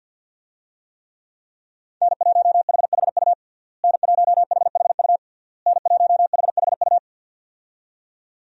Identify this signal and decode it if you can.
Morse code, heard as a keyed tone, about 53 wpm, 700 Hz. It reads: N1HHU N1HHU N1HHU
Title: USA Call Signs - 5 Characters - Repeated 3x - 50wpm